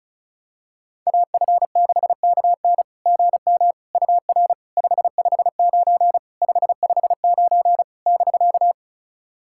Morse code, heard as a keyed tone, about 34 wpm, 690 Hz. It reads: AF6KN GM UR 559 559 <BK>
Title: SOTA QSO Practice 35wpm